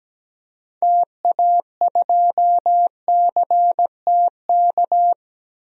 Morse code, heard as a keyed tone, about 17 wpm, 700 Hz. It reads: TA2CTK
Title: DXCC-100 Call Signs - 6 Characters - Repeated 3x - 17wpm